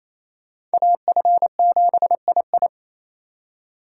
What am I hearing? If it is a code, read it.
Morse code, heard as a keyed tone, about 35 wpm, 700 Hz. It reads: AF7II
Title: Speed Racing - USA 2x2 Call Signs 28wpm